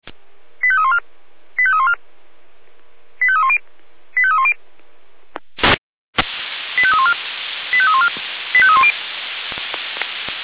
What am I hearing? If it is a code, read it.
Five-tone selective call (ZVEI), heard as quick runs of five tones, tones of about 70 ms. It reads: 85314 85314 85310 85310 85314 85314 85310